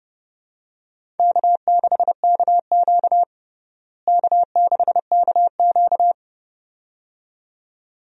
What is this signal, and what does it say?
Morse code, heard as a keyed tone, about 32 wpm, 700 Hz. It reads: K6KQ K6KQ
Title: USA Call Signs - 4 Characters - Repeated 2x - 30wpm